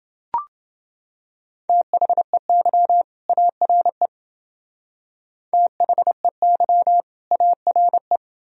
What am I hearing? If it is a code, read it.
Morse code, heard as a keyed tone, about 28 wpm, 700 Hz. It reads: THEY ARE THEY ARE